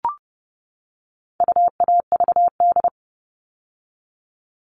Morse code, heard as a keyed tone, about 30 wpm, 700 Hz. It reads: UAVD
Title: All Cut Numbers - 4-Digit 30wpm